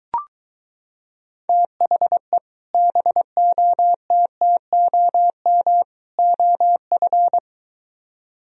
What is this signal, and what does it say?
Morse code, heard as a keyed tone, about 23 wpm, 700 Hz. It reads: THE BOTTOM OF